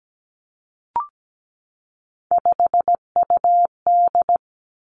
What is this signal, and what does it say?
Morse code, heard as a keyed tone, about 17 wpm, 700 Hz. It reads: 5UD
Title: Three Challenging Letter-Number Combinations 17wpm